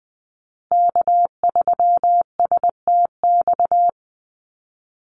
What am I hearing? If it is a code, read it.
Morse code, heard as a keyed tone, about 20 wpm, 700 Hz. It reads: K3STX
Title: Call Signs Repeated 3x - 20wpm